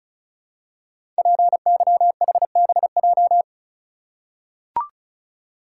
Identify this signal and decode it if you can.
Morse code, heard as a keyed tone, about 36 wpm, 700 Hz. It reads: PYHBJ
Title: Code Groups - 5 Letters - 35wpm